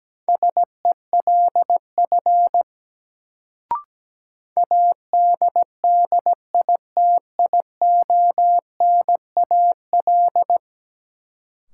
Morse code, heard as a keyed tone, about 17 wpm, 700 Hz. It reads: SELF ADDITIONAL
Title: Sentences from Top 1400 Words - New Word Drill Alt - 17wpm